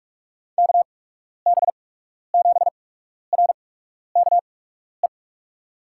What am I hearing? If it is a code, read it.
Morse code, heard as a keyed tone, about 48 wpm, spaced as for 12 wpm, 700 Hz. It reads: KB7RKE
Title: USA Call Signs - 6 Characters - 8x Character Spacing - 45wpm